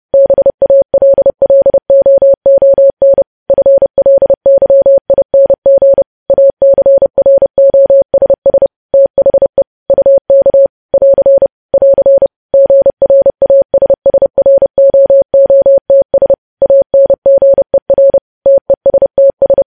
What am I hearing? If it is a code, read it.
Morse code, heard as a keyed tone, about 30 wpm, 550 Hz. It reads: BALLOON FLYING ACROSS THE UK <AR> <AR> GRASSROOTS ANGER TESTS
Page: news headlines in morse code at 30 wpm